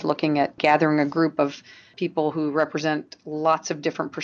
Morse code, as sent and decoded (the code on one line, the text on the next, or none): none